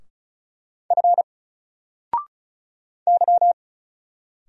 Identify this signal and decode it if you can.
Morse code, heard as a keyed tone, about 36 wpm, 700 Hz. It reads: F Y
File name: Single-Letter-Number-35wpm